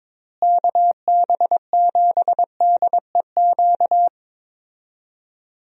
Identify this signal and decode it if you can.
Morse code, heard as a keyed tone, about 22 wpm, 700 Hz. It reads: KB7DEQ
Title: USA Call Signs - 6 Characters - Repeated 2x - 22wpm